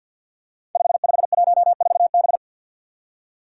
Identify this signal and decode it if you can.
Morse code, heard as a keyed tone, about 46 wpm, 700 Hz. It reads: 5514B
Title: Five Challenging Letter-Number Combinations 50wpm